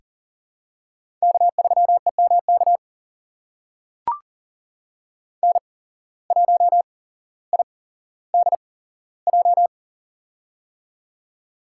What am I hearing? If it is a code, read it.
Morse code, heard as a keyed tone, about 40 wpm, spaced as for 17 wpm, 700 Hz. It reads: K3EMK N1IDJ
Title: USA Call Signs - 5 Characters - 8x Character Spacing - 40wpm